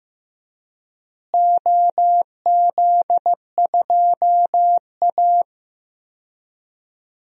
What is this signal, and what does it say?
Morse code, heard as a keyed tone, about 15 wpm, 700 Hz. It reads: OZ2A
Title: DXCC-100 Call Signs - 4 Characters - Repeated 3x - 15wpm